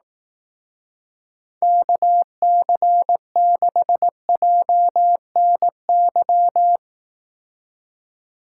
Morse code, heard as a keyed tone, about 18 wpm, 700 Hz. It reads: KC6JNY
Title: Speed Racing - USA 2x3 Call Signs 15wpm